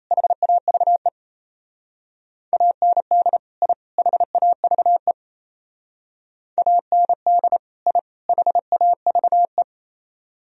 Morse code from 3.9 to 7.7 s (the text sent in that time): HAVE AND